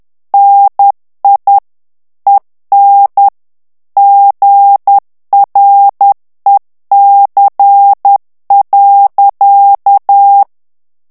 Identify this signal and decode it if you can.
Morse code, heard as a keyed tone, about 10 wpm, 800 Hz. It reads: NI EN GREC.
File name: CW / FICHIERS-MP3 / FRANCAIS TEXTE CLAIR 2 VITESSE 10 MOTS MINUTES